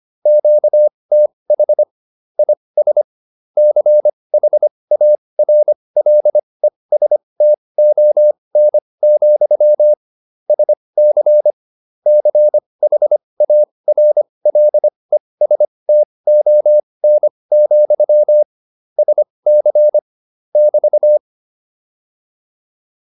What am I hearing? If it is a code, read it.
Morse code, heard as a keyed tone, about 25 wpm, spaced as for 20 wpm, 600 Hz. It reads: QTH IS CHARLESTON, SC CHARLESTON, SC <BT>
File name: QSO303a 20